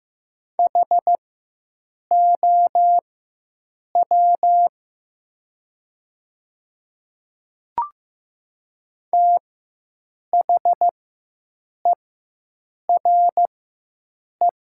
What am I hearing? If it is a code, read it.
Morse code, heard as a keyed tone, about 15 wpm, spaced as for 7 wpm, 700 Hz. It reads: HOW THERE
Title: 4x Character Spacing - Top 200 Words - 15wpm